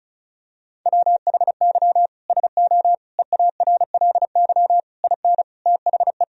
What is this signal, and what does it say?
Morse code, heard as a keyed tone, about 34 wpm, 700 Hz. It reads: WHY SO EARLY IN THE
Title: Sentences from Top 1000 Words - Review Part 2 - 35wpm